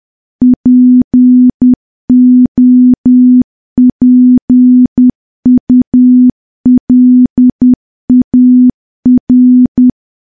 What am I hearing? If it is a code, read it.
Morse code, heard as a keyed tone, about 10 wpm, 250 Hz. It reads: POPULAR
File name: morse / morse-training-10WPM